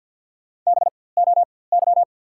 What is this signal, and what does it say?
Morse code, heard as a keyed tone, about 54 wpm, spaced as for 24 wpm, 690 Hz. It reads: BOY
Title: 4x Character Spacing - Top 500 Words - 50wpm